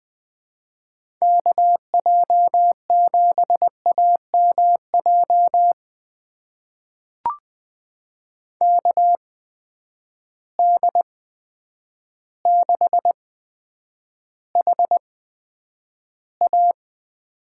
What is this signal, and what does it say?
Morse code, heard as a keyed tone, about 20 wpm, spaced as for 9 wpm, 700 Hz. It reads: KJ7AMJ KD6HA
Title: USA Call Signs - 6 Characters - 8x Character Spacing - 20wpm